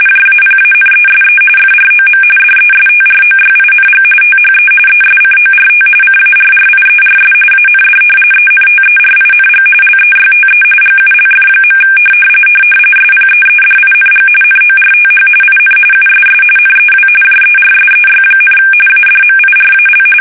RTTY, about 45.45 baud. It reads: JAPANESE FROM DESTROYING THE ESCORT CARRIERS AND ENTERING LEYTE GULF. SPEEDING NORTH IN USS NEW JERSEY, ADMIRAL
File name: fsk850 / battleshipnj